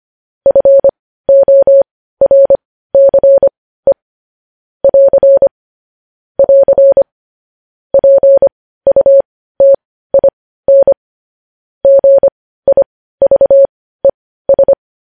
Morse code, heard as a keyed tone, about 25 wpm, spaced as for 15 wpm, 550 Hz. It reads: FORCE <AR> <AR> PUTIN GIVES
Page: news headlines in morse code at 15 wpm